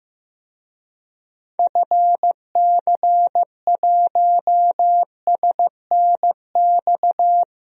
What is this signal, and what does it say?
Morse code, heard as a keyed tone, about 15 wpm, 700 Hz. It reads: FC1SNX
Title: DXCC-100 Call Signs - 6 Characters - 15wpm